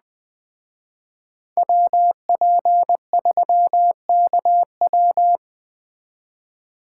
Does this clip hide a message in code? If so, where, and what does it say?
Yes: Morse code from 1.6 to 5.4 s, heard as a keyed tone, about 21 wpm, 700 Hz. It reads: WP3KW